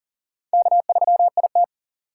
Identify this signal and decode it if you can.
Morse code, heard as a keyed tone, about 52 wpm, 700 Hz. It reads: K3IT